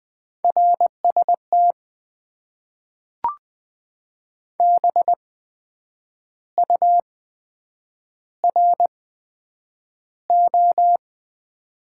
Morse code, heard as a keyed tone, about 20 wpm, spaced as for 8 wpm, 700 Hz. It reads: RST BURO
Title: Top 90 QSO Elements - 8x Character Spacing - 20wpm